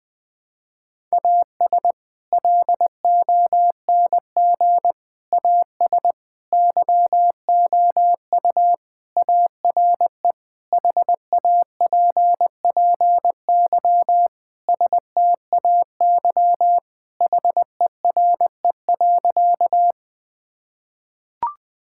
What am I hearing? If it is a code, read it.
Morse code, heard as a keyed tone, about 21 wpm, 700 Hz. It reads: AS LONG AS YOU ARE HAPPY STAY HERE.
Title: Sentences with 4-Word Phrases - 20wpm